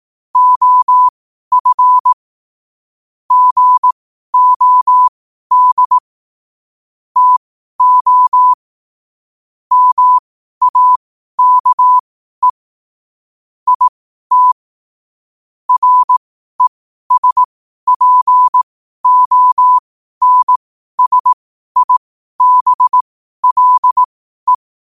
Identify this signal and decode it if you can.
Morse code, heard as a keyed tone, about 18 wpm, spaced as for 12 wpm, 1000 Hz. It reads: OF GOD TO MAKE IT RESPONSIBLE
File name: qotdfiles / 12wpm